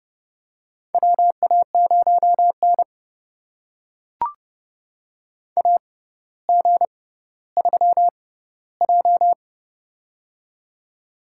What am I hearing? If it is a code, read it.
Morse code, heard as a keyed tone, about 30 wpm, spaced as for 15 wpm, 700 Hz. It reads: WA0N AG3J
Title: USA Call Signs - 4 Characters - 6x Character Spacing - 30wpm